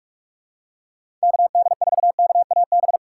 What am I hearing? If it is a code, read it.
Morse code, heard as a keyed tone, about 46 wpm, 700 Hz. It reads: KD4KAB
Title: USA Call Signs - 6 Characters - 2x Character Spacing - 45wpm